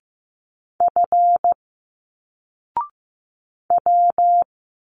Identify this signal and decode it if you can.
Morse code, heard as a keyed tone, about 15 wpm, 700 Hz. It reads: F W
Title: Single Letters 15wpm